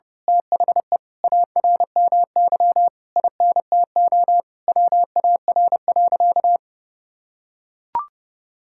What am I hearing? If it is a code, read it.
Morse code, heard as a keyed tone, about 32 wpm, 700 Hz. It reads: THE ARMY INTO WAR.
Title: Sentences from Top 1000 Words - Review Part 5 - 30wpm